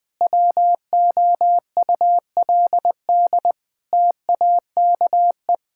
Morse code, heard as a keyed tone, about 20 wpm, 700 Hz. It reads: WOULD TAKE
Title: Sentences from Top 1600 Words 20wpm